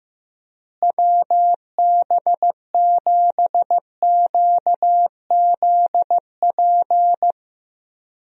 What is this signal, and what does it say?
Morse code, heard as a keyed tone, about 15 wpm, 700 Hz. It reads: WB7QZP
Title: USA Call Signs - 6 Characters - Repeated 2x - 15wpm